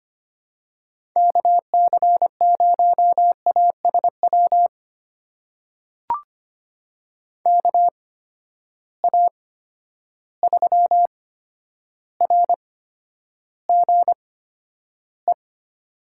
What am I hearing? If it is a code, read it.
Morse code, heard as a keyed tone, about 25 wpm, spaced as for 11 wpm, 700 Hz. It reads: KC0ASW KA3RGE
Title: USA Call Signs - 8x Character Spacing - 25wpm